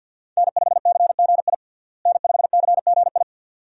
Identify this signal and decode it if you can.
Morse code, heard as a keyed tone, about 50 wpm, 700 Hz. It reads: N5CGI N5CGI
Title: USA Call Signs - 5 Characters - Repeated 2x - 50wpm